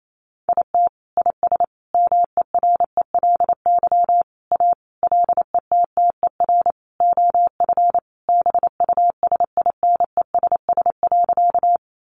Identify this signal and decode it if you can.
Morse code, heard as a keyed tone, about 28 wpm, 700 Hz. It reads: IT IS MERELY A LETTER OF BUSINESS.
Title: Sentences from Top 1400 Words 28wpm